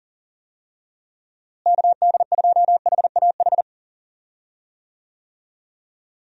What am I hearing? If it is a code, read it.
Morse code, heard as a keyed tone, about 38 wpm, 700 Hz. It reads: KD2HAH
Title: USA Call Signs - 6 Characters - Repeated 3x - 40wpm